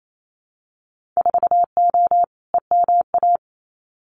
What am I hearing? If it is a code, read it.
Morse code, heard as a keyed tone, about 28 wpm, 700 Hz. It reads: 4O EMA
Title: Winter Field Day Exchanges 28wpm